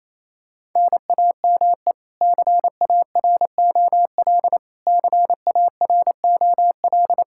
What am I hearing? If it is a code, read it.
Morse code, heard as a keyed tone, about 28 wpm, 700 Hz. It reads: NAME CAROL CAROL
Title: Practice QSO - Exchange 1 - Beginning - 28wpm